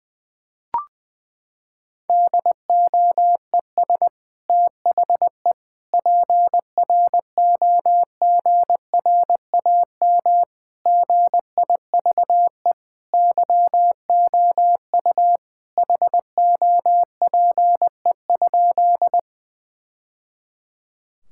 Morse code, heard as a keyed tone, about 20 wpm, 700 Hz. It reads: DOES THE PROGRAM GIVE YOU HOPE?
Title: Sentences from Top 1000 Words - Review Part 5 - 20wpm